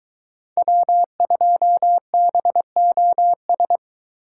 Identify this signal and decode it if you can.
Morse code, heard as a keyed tone, about 22 wpm, 700 Hz. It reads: W2BOS